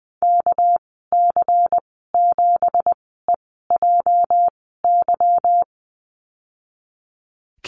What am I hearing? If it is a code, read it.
Morse code, heard as a keyed tone, about 20 wpm, spaced as for 15 wpm, 700 Hz. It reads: KC7EJY
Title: 231 Lesson 37g - Letter x - Call Signs 20wpm